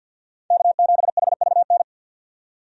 Morse code, heard as a keyed tone, about 50 wpm, 700 Hz. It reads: K7HVN
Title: USA Call Signs - 5 Characters - 50wpm